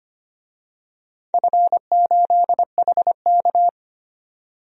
Morse code, heard as a keyed tone, about 25 wpm, 700 Hz. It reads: F8HK
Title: DXCC-100 Call Signs - 4 Characters - 25wpm